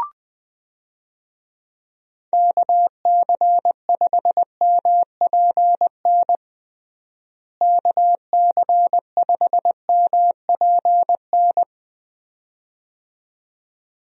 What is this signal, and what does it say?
Morse code, heard as a keyed tone, about 21 wpm, 700 Hz. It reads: KC5MPN KC5MPN
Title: USA Call Signs - 6 Characters - Repeated 2x - 20wpm